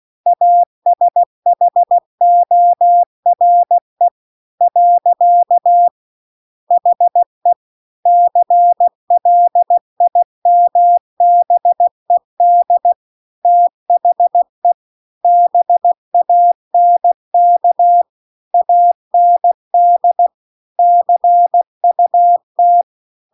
Morse code, heard as a keyed tone, about 16 wpm, 700 Hz. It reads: ASHORE . HE CLIMBED THE BANK AND CUT
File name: CW-The-Big-Twoheaded-River-2-16WPM